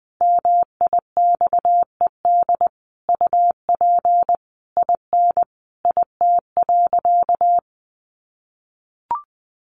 Morse code, heard as a keyed tone, about 20 wpm, 700 Hz. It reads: MIXED UP IN IT.
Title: Sentences from Top 1500 Words 20wpm